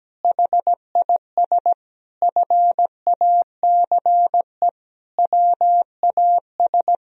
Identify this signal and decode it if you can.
Morse code, heard as a keyed tone, about 17 wpm, 700 Hz. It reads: HIS FACE WAS
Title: Sentences from Top 1000 Words - Review Part 3 - 17wpm